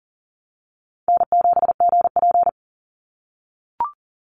Morse code, heard as a keyed tone, about 40 wpm, 700 Hz. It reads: N7GP
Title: USA Call Signs 40wpm